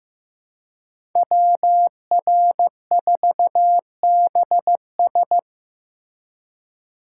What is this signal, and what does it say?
Morse code, heard as a keyed tone, about 15 wpm, 700 Hz. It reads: WR4BS
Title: USA Call Signs - 5 Characters - 4x Character Spacing - 15wpm